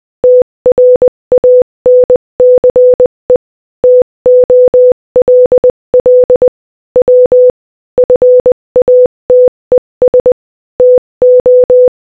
Morse code, heard as a keyed tone, about 20 wpm, 500 Hz. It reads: TRANCE TOLL W FATES TO
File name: Morse-Code-Audio-Speed-20-15